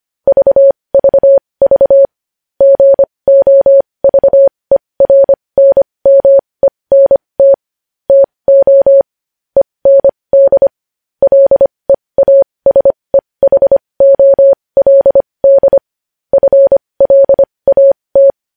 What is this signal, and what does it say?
Morse code, heard as a keyed tone, about 25 wpm, spaced as for 20 wpm, 550 Hz. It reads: VVV GOVERNMENT TO END LEASEHOLD FLAT
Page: news-headlines-in-morse-code-at-20-wpm